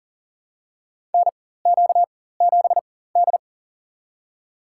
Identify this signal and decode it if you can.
Morse code, heard as a keyed tone, about 40 wpm, spaced as for 19 wpm, 700 Hz. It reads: NQ7D
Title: USA Call Signs - 4 Characters - 4x Character Spacing - 40wpm